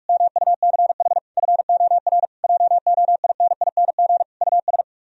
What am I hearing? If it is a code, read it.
Morse code, heard as a keyed tone, about 50 wpm, 700 Hz. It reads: MUCH FOR JOINING US